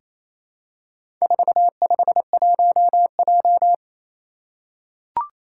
Morse code, heard as a keyed tone, about 28 wpm, 700 Hz. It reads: 451J
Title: Four Challenging Letter-Number Combinations 28wpm